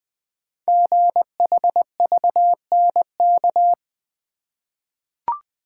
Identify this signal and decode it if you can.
Morse code, heard as a keyed tone, about 20 wpm, 700 Hz. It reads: GHVNK
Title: Code Groups - 5 Letters - 20wpm